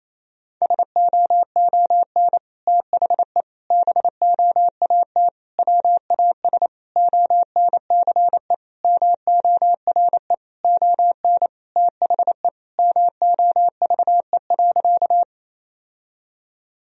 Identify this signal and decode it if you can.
Morse code, heard as a keyed tone, about 28 wpm, 700 Hz. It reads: SOON THE BOAT WAS ONCE MORE ON THE MOVE.
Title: Sentences from Top 1000 Words - Review Part 3 - 28wpm